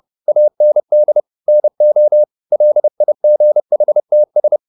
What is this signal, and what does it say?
Morse code, heard as a keyed tone, about 30 wpm, 600 Hz. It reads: AND NO LIGHTS